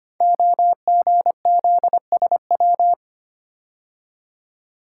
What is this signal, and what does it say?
Morse code, heard as a keyed tone, about 25 wpm, 700 Hz. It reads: OGZSW
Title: Code Groups - 5 Letter:Numbers - 25wpm